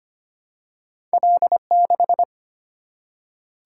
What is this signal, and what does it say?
Morse code, heard as a keyed tone, about 25 wpm, 690 Hz. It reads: L6
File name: Two Letter-Number Combinations - 25wpm